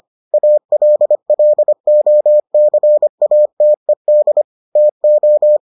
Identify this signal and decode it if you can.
Morse code, heard as a keyed tone, about 25 wpm, 600 Hz. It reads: ALLOCATED TO